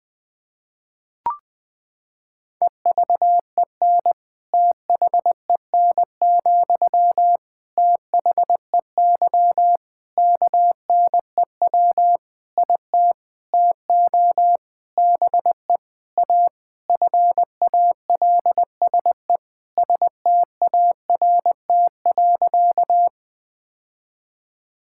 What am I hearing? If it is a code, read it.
Morse code, heard as a keyed tone, about 20 wpm, 700 Hz. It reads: EVEN THEN, THEY KNEW IT TO BE A FALSE START.
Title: Sentences from Top 1600 Words 20wpm